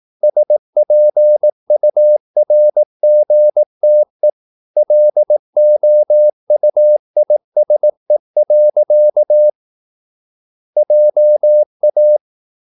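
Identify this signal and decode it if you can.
Morse code, heard as a keyed tone, about 18 wpm, 600 Hz. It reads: SPURGTE LOUISE. JA
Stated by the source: Slow (QRS) Morse practice at 18 wpm, CW med Gnister.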